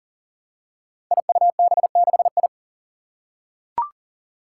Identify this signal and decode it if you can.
Morse code, heard as a keyed tone, about 38 wpm, 690 Hz. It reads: IUB6I
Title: Five Challenging Letter-Number Combinations 40wpm